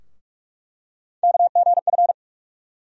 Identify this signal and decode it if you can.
Morse code, heard as a keyed tone, about 47 wpm, 700 Hz. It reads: KGF
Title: USA Call Sign Postfixes - 3 Letters 45wpm